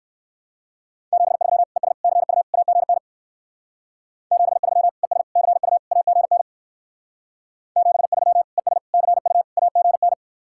Morse code, heard as a keyed tone, about 59 wpm, 700 Hz. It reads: 73 ES CU AGN 73 ES CU AGN 73 ES CU AGN